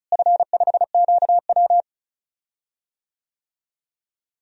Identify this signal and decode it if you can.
Morse code, heard as a keyed tone, about 34 wpm, 700 Hz. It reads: F5QW
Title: DXCC-100 Call Signs - 4 Characters - Repeated 3x - 35wpm